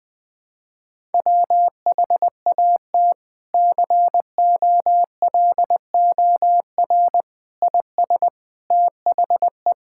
Morse code, heard as a keyed tone, about 21 wpm, 700 Hz. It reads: WHAT COLOR IS THE
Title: Sentences from Top 1000 Words - Review Part 6 - 20wpm